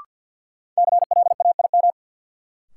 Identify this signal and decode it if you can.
Morse code, heard as a keyed tone, about 51 wpm, 700 Hz. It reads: CLAIM
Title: Sentences from Top 1500 Words - New Word Drill Alt - 50wpm